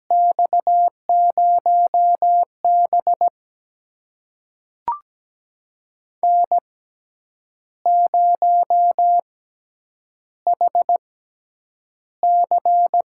Morse code, heard as a keyed tone, about 17 wpm, spaced as for 8 wpm, 700 Hz. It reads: X0B N0HC
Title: USA Call Signs - 6x Character Spacing - 17wpm